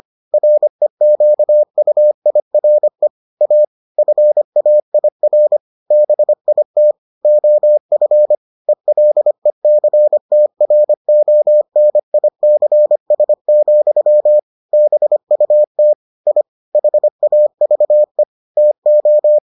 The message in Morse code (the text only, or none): REQUIRE A FAIR BIT OF ELECTRONICS, BUT I HAVE TO